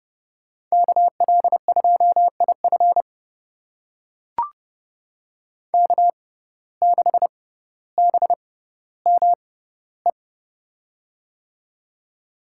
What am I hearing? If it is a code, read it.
Morse code, heard as a keyed tone, about 31 wpm, spaced as for 16 wpm, 700 Hz. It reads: KL2IF K6BME
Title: USA Call Signs - 5 Characters - 6x Character Spacing - 30wpm